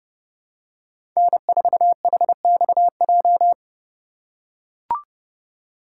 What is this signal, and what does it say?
Morse code, heard as a keyed tone, about 30 wpm, 700 Hz. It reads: N4HXJ
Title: USA Call Signs - 5 Characters - 6x Character Spacing - 30wpm